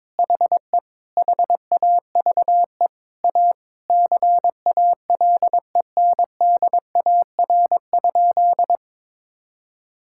Morse code, heard as a keyed tone, about 22 wpm, 700 Hz. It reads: HE HAVE A CALENDAR?
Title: Sentences from Top 1600 Words 22wpm